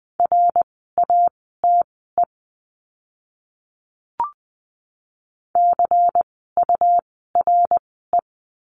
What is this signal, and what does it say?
Morse code, heard as a keyed tone, about 20 wpm, spaced as for 14 wpm, 700 Hz. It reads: RATE CURE
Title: 101 Lesson 17f - Letter u - Words 20wpm